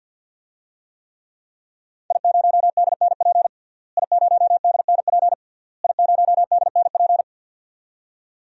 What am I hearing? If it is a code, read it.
Morse code, heard as a keyed tone, about 48 wpm, 700 Hz. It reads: I0DNP I0DNP I0DNP
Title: DXCC-100 Call Signs - 5 Characters - Repeated 3x - 50wpm